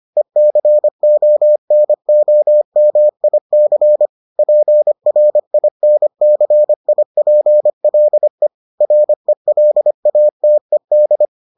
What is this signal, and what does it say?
Morse code, heard as a keyed tone, about 25 wpm, 600 Hz. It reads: ECONOMIC PRINCIPLE RELATED